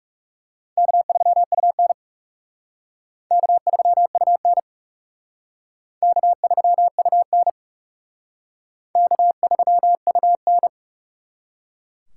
Morse code, heard as a keyed tone, about 37 wpm, 700 Hz. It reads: K3UN K3UN K3UN K3UN
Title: Speed Racing - USA 1x2 Call Signs 30wpm